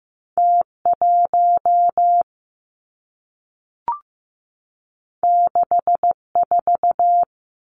Morse code, heard as a keyed tone, about 15 wpm, 700 Hz. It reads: T 1 6 4
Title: Two Challenging Letter-Number Combinations 15wpm